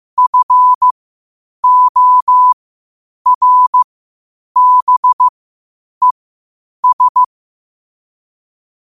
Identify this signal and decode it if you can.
Morse code, heard as a keyed tone, about 15 wpm, spaced as for 8 wpm, 1000 Hz. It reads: FORBES
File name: qotdfiles / 8wpm